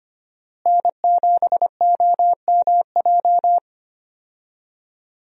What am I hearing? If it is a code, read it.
Morse code, heard as a keyed tone, about 25 wpm, 700 Hz. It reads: N7OMJ
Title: USA Call Signs - 5 Characters - Repeated 2x - 25wpm